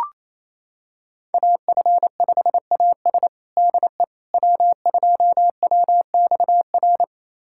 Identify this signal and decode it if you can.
Morse code, heard as a keyed tone, about 27 wpm, 700 Hz. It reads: AF5AS DE W2WXR
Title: USA Callsign DE Callsign - 5 Characters - 28wpm